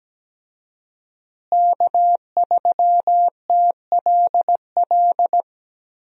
Morse code, heard as a keyed tone, about 17 wpm, 700 Hz. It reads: K3TLL
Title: USA Call Signs - 5 Characters - Repeated 3x - 17wpm